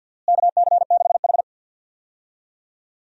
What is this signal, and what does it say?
Morse code, heard as a keyed tone, about 51 wpm, 700 Hz. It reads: KC6H
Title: USA Call Signs - 4 Characters - Repeated 2x - 50wpm